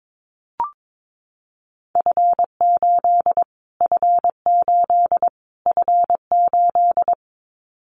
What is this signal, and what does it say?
Morse code, heard as a keyed tone, about 22 wpm, 700 Hz. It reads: F8 F8 F8
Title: Callsign Prefixes for Top 25 DXCC - Repeated 3x - 22wpm